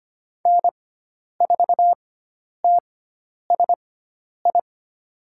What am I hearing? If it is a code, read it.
Morse code, heard as a keyed tone, about 25 wpm, spaced as for 10 wpm, 700 Hz. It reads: N4TSI
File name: Lesson 16g - Letter c - Call Signs 25wpm